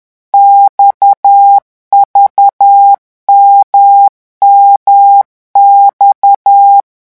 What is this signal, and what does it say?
Morse code, heard as a keyed tone, about 10 wpm, 800 Hz. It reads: XVMMX
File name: morse mp3 / revision2-vitesse 10 mots